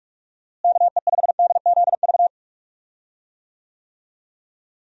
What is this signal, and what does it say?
Morse code, heard as a keyed tone, about 46 wpm, 700 Hz. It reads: KE5DZV
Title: USA Call Signs - 6 Characters - Repeated 2x - 45wpm